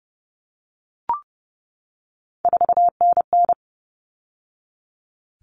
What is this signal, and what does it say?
Morse code, heard as a keyed tone, about 30 wpm, 700 Hz. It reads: 4NN
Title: All Cut Numbers - 3-Digit 30wpm